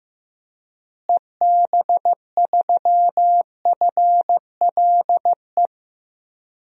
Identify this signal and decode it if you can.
Morse code, heard as a keyed tone, about 15 wpm, 700 Hz. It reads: EB3FLE
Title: DXCC-100 Call Signs - 6 Characters - Repeated 2x - 15wpm